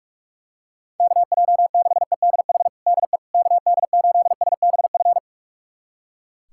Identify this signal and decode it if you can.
Morse code, heard as a keyed tone, about 44 wpm, 700 Hz. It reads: KJ6EDH DE KD8SBF
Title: USA Callsign DE Callsign - 6 Characters - 45wpm